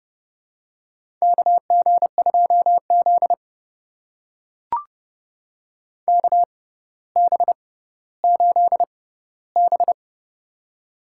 Morse code, heard as a keyed tone, about 30 wpm, spaced as for 15 wpm, 700 Hz. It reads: KG2Z KB8B
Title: USA Call Signs - 4 Characters - 6x Character Spacing - 30wpm